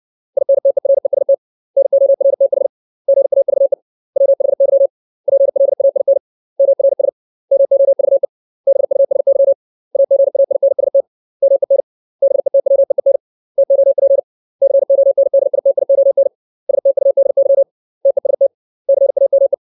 Morse code, heard as a keyed tone, about 60 wpm, 550 Hz. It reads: INTEREST NORTH GAVE WHY PLAIN MRS MOVE BUSY AGAINST MEN BETWEEN TOP CONDITION STUDY TEST CAME
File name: 632 words / 632 Words 60WPM